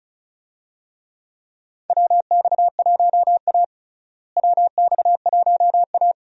Morse code, heard as a keyed tone, about 36 wpm, 700 Hz. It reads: WX1A WX1A
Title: USA Call Signs - 4 Characters - Repeated 2x - 35wpm